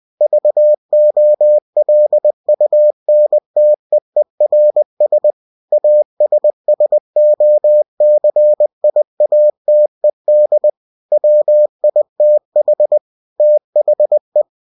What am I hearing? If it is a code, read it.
Morse code, heard as a keyed tone, about 20 wpm, 600 Hz. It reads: VOLUNTEERS ASSOCIATED WITH THE
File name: cw-books / winnie-the-pooh / winnie-20wpm-page-0120